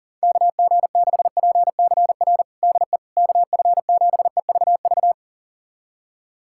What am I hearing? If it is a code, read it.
Morse code, heard as a keyed tone, about 40 wpm, 700 Hz. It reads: KG6PCR DE KF7EVV
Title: USA Callsign DE Callsign - 6 Characters - 40wpm